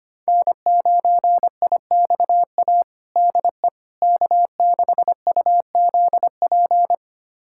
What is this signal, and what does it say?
Morse code, heard as a keyed tone, about 25 wpm, 700 Hz. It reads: N9IXA DE K6UZP
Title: USA Callsign DE Callsign - 5 Characters - 25wpm